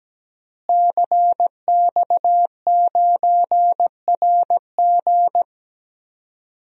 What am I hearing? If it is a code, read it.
Morse code, heard as a keyed tone, about 17 wpm, 700 Hz. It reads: CX9RG
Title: DXCC-100 Call Signs - 5 Characters - Repeated 2x - 17wpm